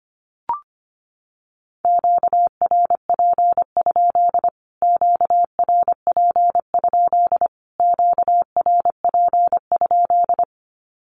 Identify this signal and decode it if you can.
Morse code, heard as a keyed tone, about 25 wpm, 700 Hz. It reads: QRP? QRP? QRP?
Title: Q-Codes Repeated 3x 25wpm